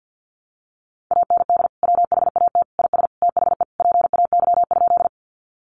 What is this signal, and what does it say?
Morse code, heard as a keyed tone, about 50 wpm, 700 Hz. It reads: AND WHAT IS THE PAY?
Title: Sentences from Top 700 Words 50wpm